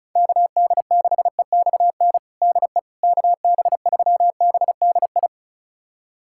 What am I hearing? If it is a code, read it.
Morse code, heard as a keyed tone, about 36 wpm, 700 Hz. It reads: KD6EXN DE KB3BDI
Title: USA Callsign DE Callsign 35wpm